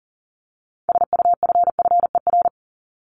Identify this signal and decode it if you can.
Morse code, heard as a keyed tone, about 40 wpm, 700 Hz. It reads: SUFFER
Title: ING Suffix - Top 100 Words with Base Words 40wpm